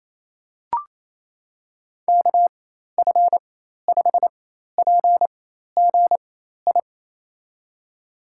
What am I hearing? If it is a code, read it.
Morse code, heard as a keyed tone, about 27 wpm, spaced as for 13 wpm, 700 Hz. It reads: KF5PGI
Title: USA Call Signs - 6 Characters - 4x Character Spacing - 28wpm